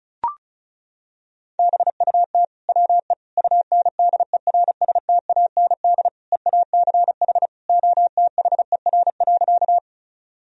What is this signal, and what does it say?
Morse code, heard as a keyed tone, about 36 wpm, 700 Hz. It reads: BUT WE UNDERSTAND EACH OTHER.
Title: Sentences from Top 1000 Words - Review Part 1 - 35wpm